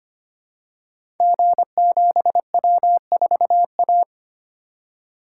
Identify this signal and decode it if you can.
Morse code, heard as a keyed tone, about 25 wpm, 700 Hz. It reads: G7W4A
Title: Code Groups - 5 Letter:Numbers - 25wpm